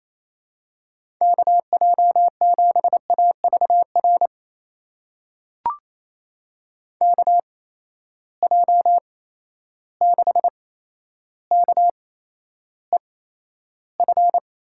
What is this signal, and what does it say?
Morse code, heard as a keyed tone, about 27 wpm, spaced as for 12 wpm, 700 Hz. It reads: KJ7AVR KJ6KEF
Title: USA Call Signs - 6 Characters - 8x Character Spacing - 28wpm